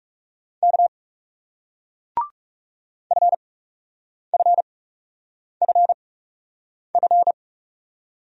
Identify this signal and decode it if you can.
Morse code, heard as a keyed tone, about 42 wpm, spaced as for 8 wpm, 700 Hz. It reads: K FFFF